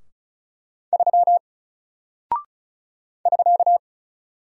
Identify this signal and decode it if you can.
Morse code, heard as a keyed tone, about 36 wpm, 700 Hz. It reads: 3 <SK>
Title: Single Character (Letter:Number:Punctuation:Prosign) 35wpm